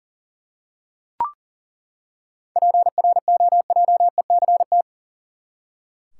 Morse code, heard as a keyed tone, about 40 wpm, 700 Hz. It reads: PROJECT
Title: Seven Letter Words - 40wpm